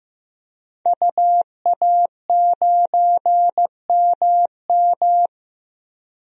Morse code, heard as a keyed tone, about 15 wpm, 700 Hz. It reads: UA9MM
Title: DXCC-100 Call Signs - 5 Characters - 15wpm